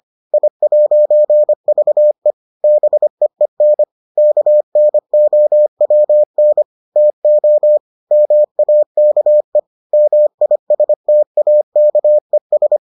Morse code, heard as a keyed tone, about 25 wpm, 600 Hz. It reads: I'VE BEEN KNOWN TO MAKE MISTAKES